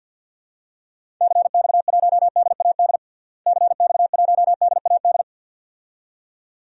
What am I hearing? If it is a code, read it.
Morse code, heard as a keyed tone, about 52 wpm, 700 Hz. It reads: CX1DAD CX1DAD
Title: DXCC-100 Call Signs - 6 Characters - Repeated 2x - 50wpm